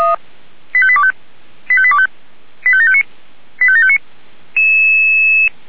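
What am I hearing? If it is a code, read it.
Five-tone selective call (ZVEI), heard as quick runs of five tones, tones of about 70 ms. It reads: 85825 85825 85759 85759
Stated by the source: Five-tone ZVEI selective call 85730 received via SDR.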